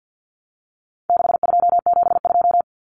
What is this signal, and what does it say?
Morse code, heard as a keyed tone, about 49 wpm, 700 Hz. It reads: 6272
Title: Common Cut Numbers - 4-Digit 50wpm